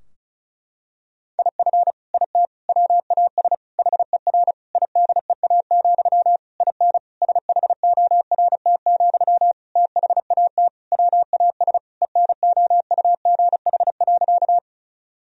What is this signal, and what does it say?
Morse code, heard as a keyed tone, about 34 wpm, 700 Hz. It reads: IF IT WAS HER IDEA, IN SHORT, THAT WAS ENOUGH.
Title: Sentences from Top 1000 Words - Review Part 2 - 35wpm